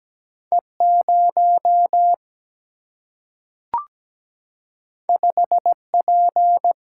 Morse code, heard as a keyed tone, about 17 wpm, 700 Hz. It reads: E 0 5 P
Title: Single-Letter Single-Number 17wpm